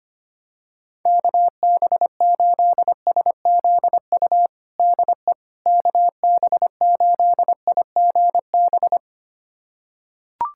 Morse code, heard as a keyed tone, about 25 wpm, 700 Hz. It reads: KB8SZU DE KB8IGB
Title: USA Callsign DE Callsign - 6 Characters - 25wpm